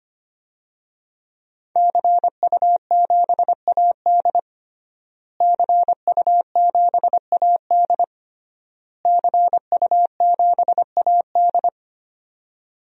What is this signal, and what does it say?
Morse code, heard as a keyed tone, about 25 wpm, 700 Hz. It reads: CU7AD CU7AD CU7AD
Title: DXCC-100 Call Signs - 5 Characters - Repeated 3x - 25wpm